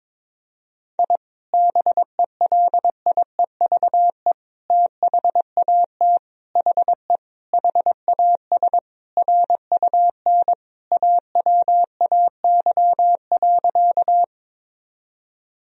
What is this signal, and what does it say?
Morse code, heard as a keyed tone, about 22 wpm, 700 Hz. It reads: I BELIEVE THAT HE HAS RUN AWAY.
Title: Sentences from Top 1000 Words - Review Part 1 - 22wpm